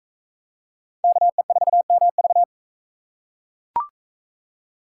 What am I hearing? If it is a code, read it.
Morse code, heard as a keyed tone, about 41 wpm, 700 Hz. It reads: KE4MV